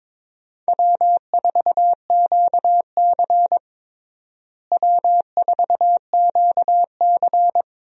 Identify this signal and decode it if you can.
Morse code, heard as a keyed tone, about 22 wpm, 700 Hz. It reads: W4QC W4QC
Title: USA Call Signs - 4 Characters - Repeated 2x - 22wpm